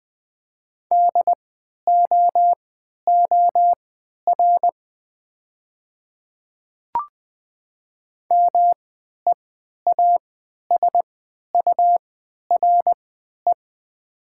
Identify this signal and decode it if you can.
Morse code, heard as a keyed tone, about 20 wpm, spaced as for 11 wpm, 700 Hz. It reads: DOOR MEASURE
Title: 3x Character Spacing - Top 500 Words - 20wpm